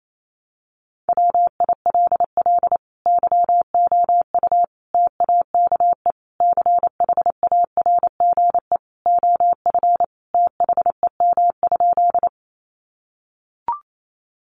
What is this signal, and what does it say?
Morse code, heard as a keyed tone, about 28 wpm, 700 Hz. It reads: WILL YOU TAKE CHARGE OF THEM?
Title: Sentences from Top 900 Words 28wpm